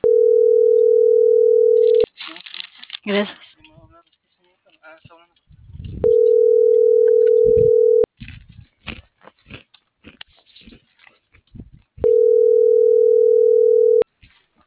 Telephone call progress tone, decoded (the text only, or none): ringback tone